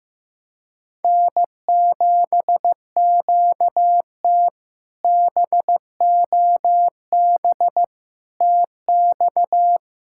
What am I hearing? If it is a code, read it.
Morse code, heard as a keyed tone, about 16 wpm, 700 Hz. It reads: N7QT BOB TX
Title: K1USN SST QSO Practice - 15wpm